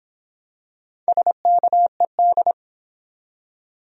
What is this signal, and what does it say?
Morse code, heard as a keyed tone, about 23 wpm, 700 Hz. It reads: SKED